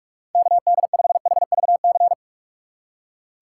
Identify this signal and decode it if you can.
Morse code, heard as a keyed tone, about 40 wpm, 700 Hz. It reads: KD5HVC